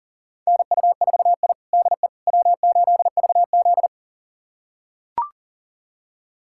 Morse code, heard as a keyed tone, about 37 wpm, 700 Hz. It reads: NU4I DE W8VZ
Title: USA Callsign DE Callsign - 4 Characters - 40wpm